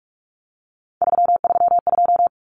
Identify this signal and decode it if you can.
Morse code, heard as a keyed tone, about 44 wpm, 700 Hz. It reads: 332
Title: Numbers - 3-Digit 45wpm